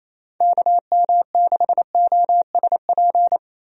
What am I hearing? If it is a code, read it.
Morse code, heard as a keyed tone, about 27 wpm, 700 Hz. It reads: KM6OSP
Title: USA Call Signs - 6 Characters - Repeated 2x - 28wpm